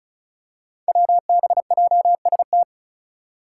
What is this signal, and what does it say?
Morse code, heard as a keyed tone, about 34 wpm, 700 Hz. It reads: WBJST
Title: Five Challenging Letter-Number Combinations 35wpm